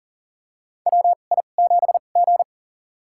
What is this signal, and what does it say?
Morse code, heard as a keyed tone, about 37 wpm, spaced as for 28 wpm, 700 Hz. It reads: WI7G
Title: USA Call Signs - 4 Characters - 2x Character Spacing - 40wpm